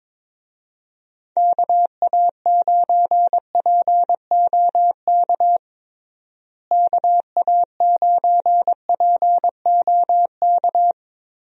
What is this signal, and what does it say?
Morse code, heard as a keyed tone, about 22 wpm, 700 Hz. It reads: KA9POK KA9POK
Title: USA Call Signs - 6 Characters - Repeated 3x - 22wpm